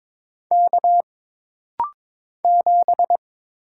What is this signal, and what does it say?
Morse code, heard as a keyed tone, about 22 wpm, 700 Hz. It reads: K 7
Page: This rapid-fire timing is geared to push your ICR (Instant Character Recognition) skills.